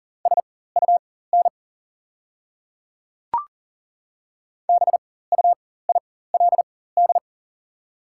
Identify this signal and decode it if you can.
Morse code, heard as a keyed tone, about 40 wpm, spaced as for 19 wpm, 690 Hz. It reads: SUN BUILD
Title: 4x Character Spacing - Top 300 Words - 40wpm